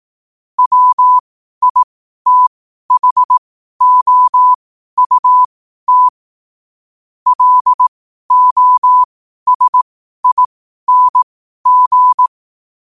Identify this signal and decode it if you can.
Morse code, heard as a keyed tone, about 18 wpm, spaced as for 12 wpm, 1000 Hz. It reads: WITHOUT LOSING
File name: qotdfiles / 12wpm